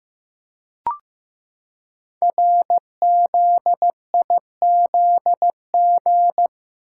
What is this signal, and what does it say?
Morse code, heard as a keyed tone, about 15 wpm, 700 Hz. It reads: RZIZG
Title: Code Groups - 5 Letters - 15wpm